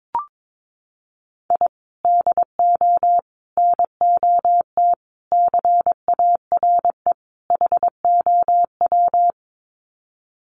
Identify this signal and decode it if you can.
Morse code, heard as a keyed tone, about 22 wpm, 700 Hz. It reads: I DO NOT CARE HOW
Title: Sets of 5 Words Encore2 22wpm